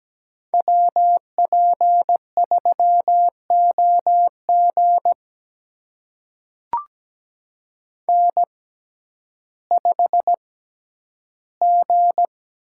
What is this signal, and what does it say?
Morse code, heard as a keyed tone, about 17 wpm, spaced as for 10 wpm, 700 Hz. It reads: WP3OG N5G
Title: USA Call Signs - 6x Character Spacing - 17wpm